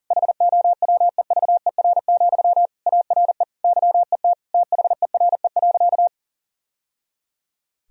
Morse code, heard as a keyed tone, about 40 wpm, 700 Hz. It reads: HOWEVER, ARE YET THERE.
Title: Sentences from Top 1000 Words - Review Part 1 - 40wpm